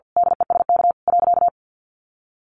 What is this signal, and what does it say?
Morse code, heard as a keyed tone, about 50 wpm, 700 Hz. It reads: DESK .
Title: A Sound Of Thunder 50wpm